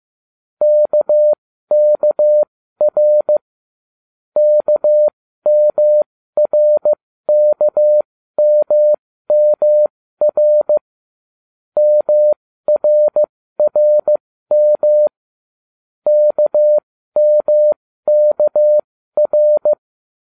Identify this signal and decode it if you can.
Morse code, heard as a keyed tone, about 15 wpm, spaced as for 13 wpm, 600 Hz. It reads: KKR KMRKMMR MRRM KMKR